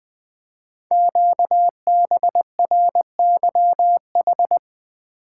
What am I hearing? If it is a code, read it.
Morse code, heard as a keyed tone, about 20 wpm, 700 Hz. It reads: QBRYH
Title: Code Groups - 5 Letters - 20wpm